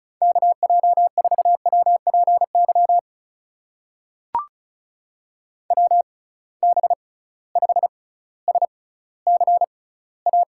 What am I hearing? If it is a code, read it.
Morse code, heard as a keyed tone, about 36 wpm, spaced as for 18 wpm, 700 Hz. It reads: KJ4WPY WB5SCA
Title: USA Call Signs - 6 Characters - 6x Character Spacing - 35wpm